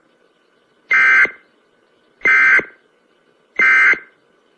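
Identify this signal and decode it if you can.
Emergency Alert System (SAME) tones, heard as harsh buzzing bursts. It is heard from 0.9 to 4.1 s.
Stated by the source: Emergency broadcast alert tones offered as a ringtone.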